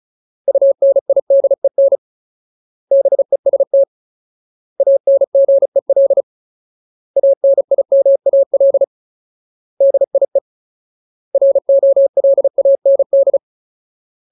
Morse code, heard as a keyed tone, about 34 wpm, 550 Hz. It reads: UNIDEN BEST ANGEL ANIMAL DIE ROLAND
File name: Ultimate Word List / 35WPM Ultimate